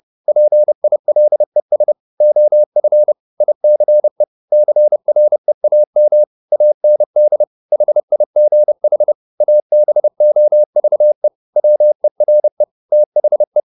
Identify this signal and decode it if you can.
Morse code, heard as a keyed tone, about 30 wpm, 600 Hz. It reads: PILES OF ICE CREAM AND HIGH ABOVE WERE THE
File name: the old man and the sea / the old man and the sea 1 30wpm